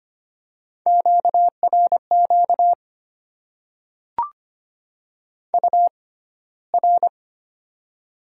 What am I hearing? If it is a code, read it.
Morse code, heard as a keyed tone, about 25 wpm, spaced as for 15 wpm, 690 Hz. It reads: QRQ UR
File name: Top 90 QSO Elements - 6x Character Spacing - 25wpm